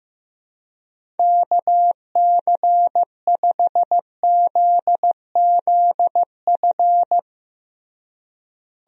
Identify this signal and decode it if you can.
Morse code, heard as a keyed tone, about 16 wpm, 700 Hz. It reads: KC5ZZF
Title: USA Call Signs - 6 Characters - Repeated 3x - 15wpm